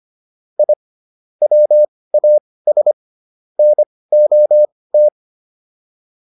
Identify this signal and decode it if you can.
Morse code, heard as a keyed tone, about 25 wpm, spaced as for 18 wpm, 600 Hz. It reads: I WAS NOT